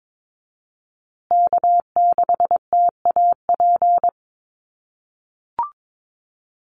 Morse code, heard as a keyed tone, about 22 wpm, 700 Hz. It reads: K6TAP
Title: USA Call Signs Repeated 2x 22wpm